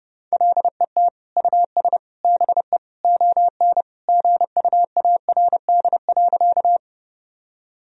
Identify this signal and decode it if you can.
Morse code, heard as a keyed tone, about 32 wpm, 700 Hz. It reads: LET US BE ON GUARD.
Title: Sentences from Top 1000 Words - Review Part 4 - 30wpm